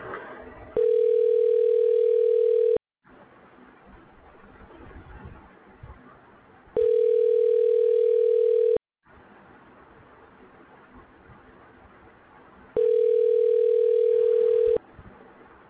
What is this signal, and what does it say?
Ringback tone (North America), 440 + 480 Hz.